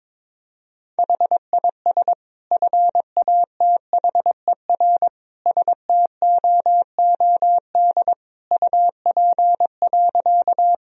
Morse code, heard as a keyed tone, about 22 wpm, 700 Hz. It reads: HIS FATHER STOOD UP.
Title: Sentences from Top 1000 Words - Review Part 2 - 22wpm